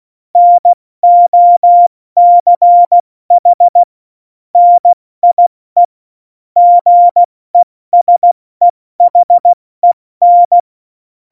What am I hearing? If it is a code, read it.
Morse code, heard as a keyed tone, about 16 wpm, 700 Hz. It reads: NOCH NIE GESEHEN
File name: Morsekurs / Witze / Witz 28 Juli 2020, 70 F80